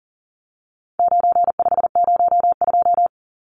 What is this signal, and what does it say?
Morse code, heard as a keyed tone, about 40 wpm, 700 Hz. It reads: 9502
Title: Numbers - 4-Digit 40wpm